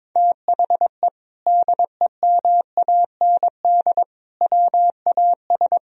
Morse code, heard as a keyed tone, about 22 wpm, 700 Hz. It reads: THE DEMAND WAS
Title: Sentences from Top 1000 Words - Review Part 4 - 22wpm